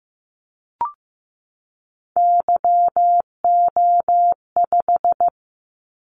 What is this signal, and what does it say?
Morse code, heard as a keyed tone, about 15 wpm, 700 Hz. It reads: YO5
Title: Callsign Prefixes for Top 25 DXCC - 15wpm